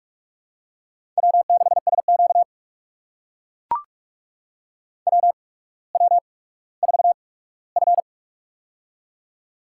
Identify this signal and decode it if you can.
Morse code, heard as a keyed tone, about 42 wpm, spaced as for 19 wpm, 700 Hz. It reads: W6SQ WW4F
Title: USA Call Signs - 4 Characters - 8x Character Spacing - 45wpm